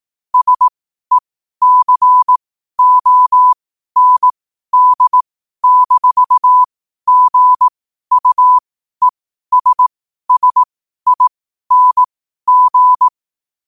Morse code, heard as a keyed tone, about 18 wpm, spaced as for 12 wpm, 1000 Hz. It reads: SECOND-GUESSING
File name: qotdfiles / 12wpm